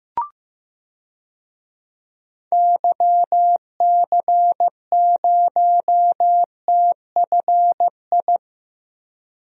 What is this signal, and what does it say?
Morse code, heard as a keyed tone, about 15 wpm, 700 Hz. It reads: YC0TFI
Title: DXCC-100 Call Signs - 6 Characters - Repeated 3x - 15wpm